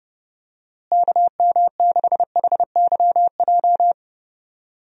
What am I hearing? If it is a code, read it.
Morse code, heard as a keyed tone, about 30 wpm, 700 Hz. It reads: KM6HYJ
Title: USA Call Signs - 6 Characters - Repeated 3x - 30wpm